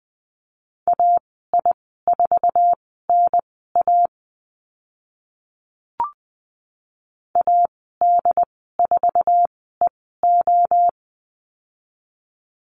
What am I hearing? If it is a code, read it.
Morse code, heard as a keyed tone, about 20 wpm, spaced as for 14 wpm, 700 Hz. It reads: AI4NA AD4EO
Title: 069 Lesson 12g - Letter d - Call Signs 20wpm